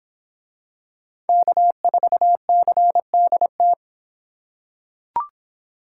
Morse code, heard as a keyed tone, about 26 wpm, 700 Hz. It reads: K4CDT